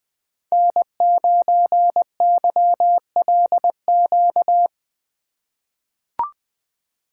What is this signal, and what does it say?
Morse code, heard as a keyed tone, about 20 wpm, 700 Hz. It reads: N9YLQ
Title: USA Call Signs - 5 Characters - 3x Character Spacing - 20wpm